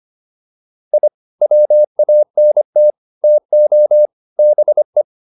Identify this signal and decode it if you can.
Morse code, heard as a keyed tone, about 25 wpm, 600 Hz. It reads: I WANT TO BE